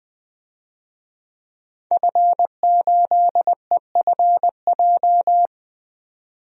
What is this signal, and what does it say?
Morse code, heard as a keyed tone, about 20 wpm, 700 Hz. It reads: F8EFJ
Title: DXCC-100 Call Signs - 5 Characters - 20wpm